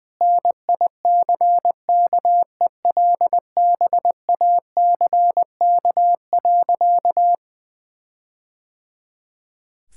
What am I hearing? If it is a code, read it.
Morse code, heard as a keyed tone, about 21 wpm, 700 Hz. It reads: NICKELBACK.
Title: All-Age Appropriate Jokes - 20wpm